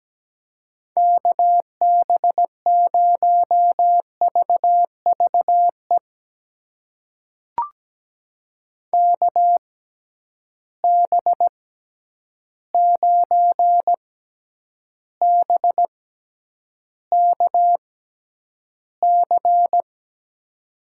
Morse code, heard as a keyed tone, about 17 wpm, spaced as for 9 wpm, 700 Hz. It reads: KB0VVE KB9BKC
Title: USA Call Signs - 6 Characters - 6x Character Spacing - 17wpm